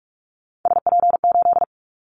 Morse code, heard as a keyed tone, about 46 wpm, 700 Hz. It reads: SP8
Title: Callsign Prefixes for Top 25 DXCC - 45wpm